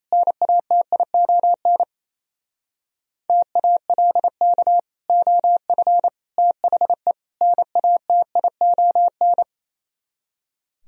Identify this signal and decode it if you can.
Morse code, heard as a keyed tone, about 35 wpm, 690 Hz. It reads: NATION TALK OF THE NATION